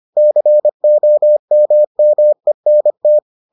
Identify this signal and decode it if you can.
Morse code, heard as a keyed tone, about 25 wpm, 600 Hz. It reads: COMMENT